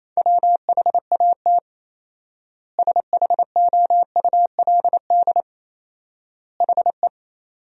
Morse code, heard as a keyed tone, about 27 wpm, 700 Hz. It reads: WHAT SHOULD HE
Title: 4x Word-Spacing - Sentences from Top 300 Words - 28wpm